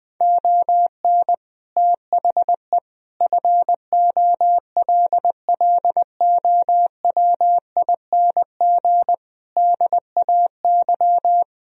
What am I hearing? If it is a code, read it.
Morse code, heard as a keyed tone, about 20 wpm, 700 Hz. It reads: ON THE FOLLOWING DAY
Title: Sentences from Top 1000 Words - Review Part 4 - 20wpm